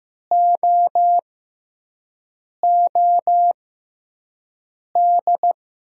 Morse code, heard as a keyed tone, about 15 wpm, spaced as for 5 wpm, 700 Hz. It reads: OOD
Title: USA Call Signs - 5 Characters - 6x Character Spacing - 15wpm